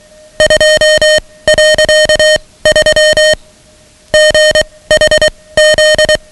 Morse code, heard as a keyed tone, about 23 wpm, spaced as for 17 wpm, 620 Hz, with noise. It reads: 2.3 GHZ
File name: ham radio / CodeQuick / Code Quick 10